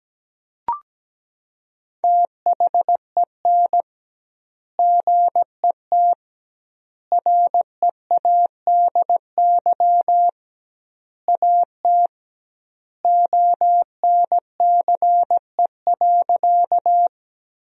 Morse code, heard as a keyed tone, about 17 wpm, 700 Hz. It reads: THEN GET READY AT ONCE.
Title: 2x Word-Spacing - Sentences from Top 400 Words - 17wpm